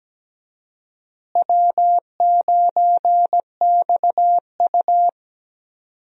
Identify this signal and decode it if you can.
Morse code, heard as a keyed tone, about 17 wpm, 700 Hz. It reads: W9XU
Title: USA Call Signs - 4 Characters - Repeated 2x - 17wpm